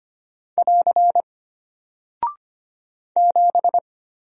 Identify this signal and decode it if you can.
Morse code, heard as a keyed tone, about 25 wpm, 700 Hz. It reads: <AR> 7
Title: Single Character (Letter:Number:Punctuation:Prosign) Alt 25wpm